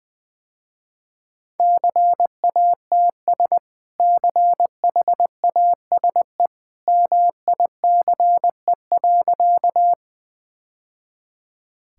Morse code, heard as a keyed tone, about 20 wpm, 700 Hz. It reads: CATS CHASE MICE.
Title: Easy Sentences with Hints - 20wpm